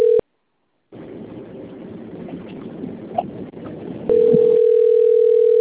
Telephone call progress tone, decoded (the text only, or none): ringback tone